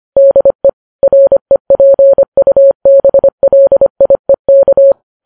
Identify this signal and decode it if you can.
Morse code, heard as a keyed tone, about 25 wpm, 550 Hz. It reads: DE REPUBLIEK
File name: Files Web tls / Morse on9fv / morse25kps